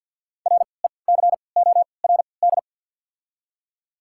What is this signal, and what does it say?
Morse code, heard as a keyed tone, about 53 wpm, spaced as for 29 wpm, 690 Hz. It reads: RECORD
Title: 3x Character Spacing - Top 500 Words - 50wpm